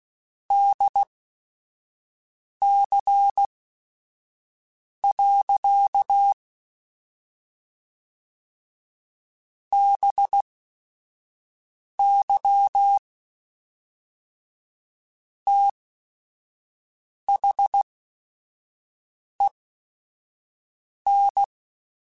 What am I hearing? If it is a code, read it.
Morse code, heard as a keyed tone, about 16 wpm, spaced as for 5 wpm, 800 Hz. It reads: DC. BY THEN
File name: Technical Subjects / Technical Subjects-mp3s / Sw Pwr Sup @5